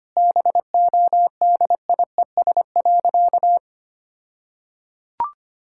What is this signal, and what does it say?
Morse code, heard as a keyed tone, about 25 wpm, 700 Hz. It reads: BODIES.
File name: practice / 25wpm / All-Age Appropriate Jokes - 25wpm